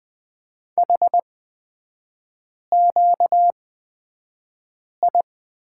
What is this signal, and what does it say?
Morse code, heard as a keyed tone, about 20 wpm, spaced as for 5 wpm, 690 Hz. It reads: HQI
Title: 236 Lesson 38e - Letter q - Three Characters 20wpm